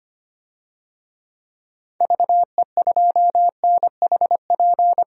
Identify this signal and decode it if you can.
Morse code, heard as a keyed tone, about 25 wpm, 700 Hz. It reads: VE2NHP
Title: DXCC-100 Call Signs - 6 Characters - 25wpm